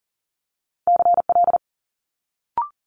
Morse code, heard as a keyed tone, about 41 wpm, 700 Hz. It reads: CL
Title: Multi-letter Phonemes Quick Fire 40wpm